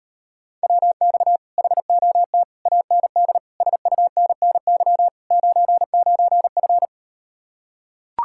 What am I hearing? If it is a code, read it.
Morse code, heard as a keyed tone, about 38 wpm, 700 Hz. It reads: WX HOT AND SUNNY 99F